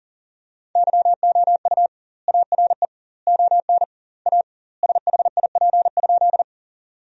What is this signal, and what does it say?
Morse code, heard as a keyed tone, about 40 wpm, 700 Hz. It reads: YOU ARE ON A SHIP?
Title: 2x Word-Spacing - Sentences from Top 400 Words - 40wpm